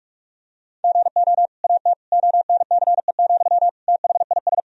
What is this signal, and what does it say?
Morse code, heard as a keyed tone, about 41 wpm, 700 Hz. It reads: GO AT ONCE, THIS
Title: Sentences from Top 1000 Words - Review Part 1 - 45wpm